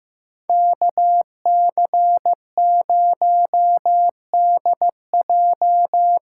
Morse code, heard as a keyed tone, about 15 wpm, 700 Hz. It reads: KC0DJ
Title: USA Call Signs - 8x Character Spacing - 15wpm